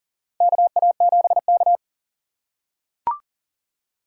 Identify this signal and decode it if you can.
Morse code, heard as a keyed tone, about 40 wpm, 700 Hz. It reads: KA7K
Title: USA Call Signs - 6x Character Spacing - 40wpm